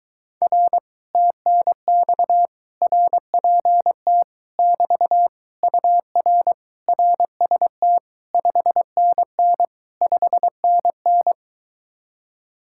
Morse code, heard as a keyed tone, about 22 wpm, 700 Hz. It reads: R TNX RPT <BT> UR RST 5NN 5NN